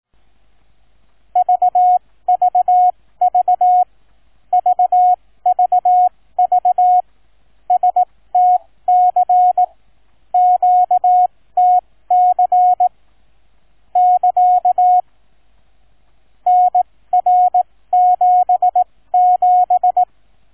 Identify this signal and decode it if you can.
Morse code, heard as a keyed tone, about 18 wpm, spaced as for 15 wpm, 730 Hz. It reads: VVV VVV STC QTC <KA> NR77